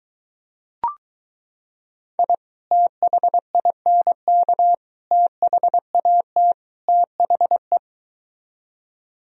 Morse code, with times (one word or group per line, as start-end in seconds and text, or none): 2.2-2.3 I
2.7-4.7 THINK
5.1-6.5 THAT
6.9-7.8 THE